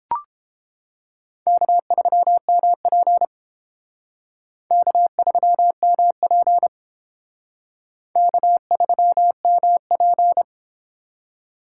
Morse code, heard as a keyed tone, about 31 wpm, 700 Hz. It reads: K3MP K3MP K3MP